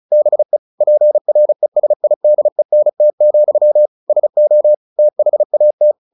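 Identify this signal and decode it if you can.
Morse code, heard as a keyed tone, about 34 wpm, 600 Hz. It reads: BE PRESIDENT, SO THAT
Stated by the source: Morse practice at 35 wpm from CW med Gnister, sending Ken Follett's novel Never in English.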